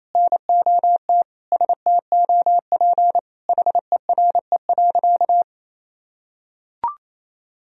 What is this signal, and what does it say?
Morse code, heard as a keyed tone, about 28 wpm, 700 Hz. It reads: NOT STOP HERE.
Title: Sentences from Top 1000 Words - Review Part 2 - 28wpm